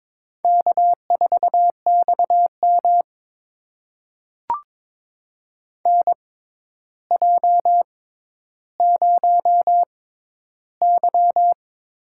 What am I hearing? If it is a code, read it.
Morse code, heard as a keyed tone, about 21 wpm, spaced as for 11 wpm, 700 Hz. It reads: K4XM NJ0Y
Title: USA Call Signs - 4 Characters - 6x Character Spacing - 22wpm